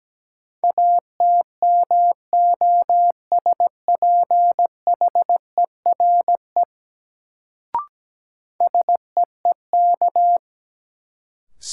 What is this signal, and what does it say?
Morse code, heard as a keyed tone, about 17 wpm, 700 Hz. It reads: ATMOSPHERE SEEK
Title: Sentences from Top 1600 Words - New Word Drill - 17wpm